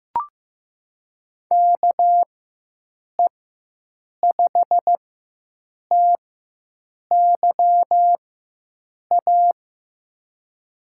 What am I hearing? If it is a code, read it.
Morse code, heard as a keyed tone, about 15 wpm, spaced as for 7 wpm, 700 Hz. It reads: KE5TYA
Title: USA Call Signs - 6 Characters - 4x Character Spacing - 15wpm